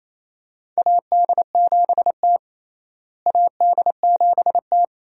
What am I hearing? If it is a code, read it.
Morse code, heard as a keyed tone, about 28 wpm, 700 Hz. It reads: AD7T AD7T
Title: USA Call Signs - 4 Characters - Repeated 2x - 28wpm